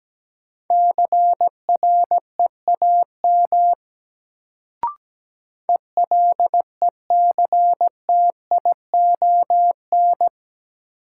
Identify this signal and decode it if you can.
Morse code, heard as a keyed tone, about 17 wpm, 700 Hz. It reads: CREAM ELECTION
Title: Sentences from Top 1500 Words - New Word Drill Alt - 17wpm